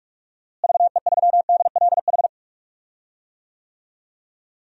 Morse code, heard as a keyed tone, about 46 wpm, 700 Hz. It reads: VE3DLH
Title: DXCC-100 Call Signs - 6 Characters - Repeated 2x - 45wpm